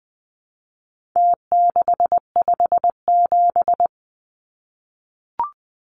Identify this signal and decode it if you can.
Morse code, heard as a keyed tone, about 20 wpm, 700 Hz. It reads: T657
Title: Numbers - Contest Style - 4-Digit 20wpm